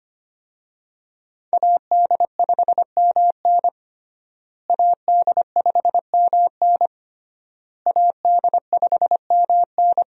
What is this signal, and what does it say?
Morse code, heard as a keyed tone, about 25 wpm, 700 Hz. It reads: AD5MN AD5MN AD5MN
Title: USA Call Signs - 5 Characters - Repeated 3x - 25wpm